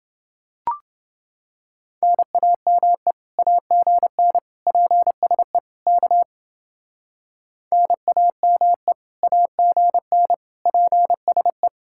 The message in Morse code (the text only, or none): NAME AGN PSE K NAME AGN PSE